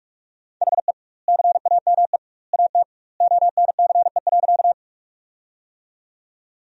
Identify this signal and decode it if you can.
Morse code, heard as a keyed tone, about 47 wpm, 700 Hz. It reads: HE CAME AT ONCE.
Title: 2x Word-Spacing - Sentences from Top 300 Words - 45wpm